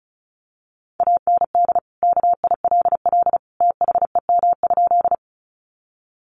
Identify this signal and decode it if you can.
Morse code, heard as a keyed tone, about 34 wpm, 700 Hz. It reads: AND KILL THEM?
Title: Sentences from Top 1400 Words 35wpm